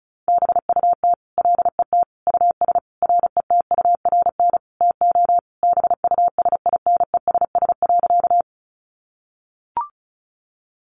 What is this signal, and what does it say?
Morse code, heard as a keyed tone, about 34 wpm, 700 Hz. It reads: BUT LET US RETURN TO BUSINESS.
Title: Sentences from Top 800 Words 35wpm